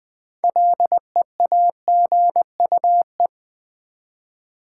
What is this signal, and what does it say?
Morse code, heard as a keyed tone, about 20 wpm, 700 Hz. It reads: LEAGUE
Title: 6 Letter Words - First 3 Letters Given - 20wpm